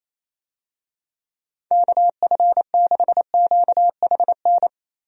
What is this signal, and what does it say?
Morse code, heard as a keyed tone, about 28 wpm, 700 Hz. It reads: KF6QHN
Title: USA Call Signs - 6 Characters - 28wpm